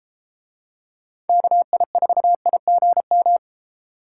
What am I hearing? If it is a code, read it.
Morse code, heard as a keyed tone, about 32 wpm, 700 Hz. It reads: KI4IGM